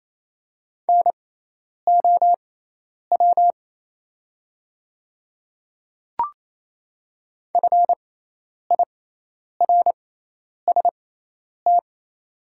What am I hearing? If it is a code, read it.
Morse code, heard as a keyed tone, about 27 wpm, spaced as for 10 wpm, 700 Hz. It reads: NOW FIRST
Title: 6x Character Spacing - Top 200 Words - 28wpm